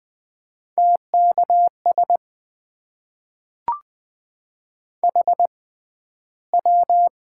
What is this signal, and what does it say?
Morse code, heard as a keyed tone, about 20 wpm, spaced as for 12 wpm, 700 Hz. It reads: TKS HW
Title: Top 90 QSO Elements - 6x Character Spacing - 20wpm